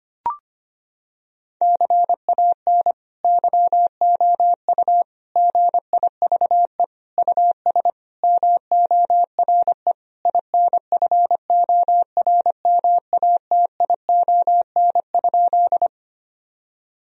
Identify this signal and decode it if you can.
Morse code, heard as a keyed tone, about 25 wpm, 700 Hz. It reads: CAN YOU GIVE US MORE INFORMATION?
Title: Sentences from Top 1000 Words - Review Part 3 - 25wpm